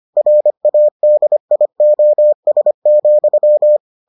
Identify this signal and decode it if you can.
Morse code, heard as a keyed tone, about 25 wpm, 600 Hz. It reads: RADIOS,